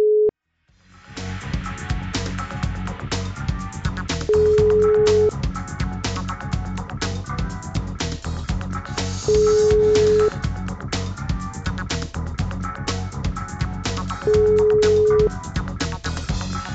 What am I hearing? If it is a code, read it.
Ringback tone (Europe, 425 Hz).